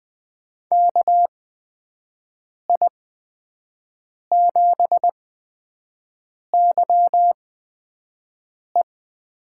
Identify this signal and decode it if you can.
Morse code, heard as a keyed tone, about 20 wpm, spaced as for 5 wpm, 700 Hz. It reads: KI7YE
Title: USA Call Signs - 8x Character Spacing - 20wpm